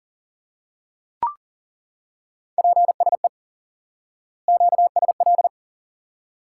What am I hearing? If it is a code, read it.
Morse code, heard as a keyed tone, about 40 wpm, 700 Hz. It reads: PSE QSL
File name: course / instant-qso-element-course-3 / IQE Course 3 - Lesson 89 - YAGI - Two Elements - 40wpm